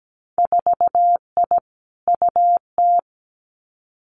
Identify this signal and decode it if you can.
Morse code, heard as a keyed tone, about 17 wpm, 700 Hz. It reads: 4I UT
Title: Winter Field Day Exchanges Encore 17wpm